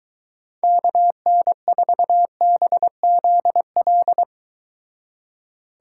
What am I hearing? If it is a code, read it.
Morse code, heard as a keyed tone, about 22 wpm, 700 Hz. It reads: KN4BZL